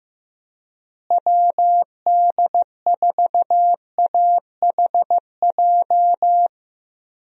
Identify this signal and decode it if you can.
Morse code, heard as a keyed tone, about 15 wpm, 700 Hz. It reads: WD4AHJ
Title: USA Call Signs - 6 Characters - Repeated 2x - 15wpm